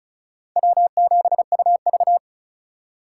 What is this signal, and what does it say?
Morse code, heard as a keyed tone, about 44 wpm, 700 Hz. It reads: W7UV